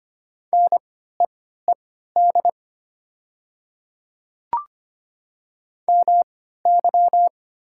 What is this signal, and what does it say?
Morse code, heard as a keyed tone, about 25 wpm, spaced as for 14 wpm, 700 Hz. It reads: NEED MY
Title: 3x Character Spacing - Top 300 Words - 25wpm